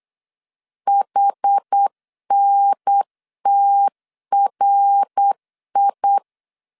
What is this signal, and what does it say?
Morse code, heard as a keyed tone, about 8 wpm, 800 Hz. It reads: HNTRI